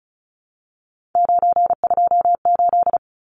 Morse code, heard as a keyed tone, about 34 wpm, 700 Hz. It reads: 928
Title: Numbers - 3-Digit 35wpm